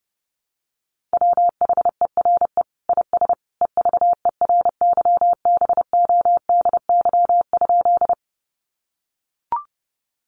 Morse code, heard as a keyed tone, about 30 wpm, 700 Hz. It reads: WHERE IS EVERYBODY?
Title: Sentences from Top 1400 Words 30wpm